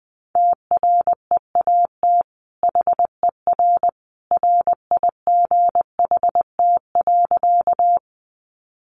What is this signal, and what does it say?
Morse code, heard as a keyed tone, about 20 wpm, 700 Hz. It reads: TREAT HER RIGHT.
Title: Sentences from Top 1300 Words 20wpm